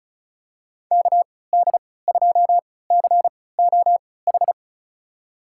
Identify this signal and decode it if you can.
Morse code, heard as a keyed tone, about 36 wpm, spaced as for 20 wpm, 700 Hz. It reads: KD2COH
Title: USA Call Signs - 3x Character Spacing - 35wpm